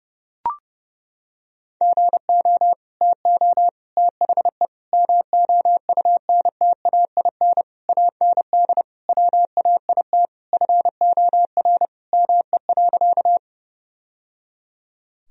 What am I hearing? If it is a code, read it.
Morse code, heard as a keyed tone, about 32 wpm, 700 Hz. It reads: GO TO THE MOUNTAIN AND WAIT FOR ME.
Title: Sentences from Top 1000 Words - Review Part 1 - 30wpm